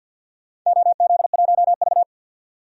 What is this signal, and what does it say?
Morse code, heard as a keyed tone, about 51 wpm, 700 Hz. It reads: OZ1V
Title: DXCC-100 Call Signs - 4 Characters - 50wpm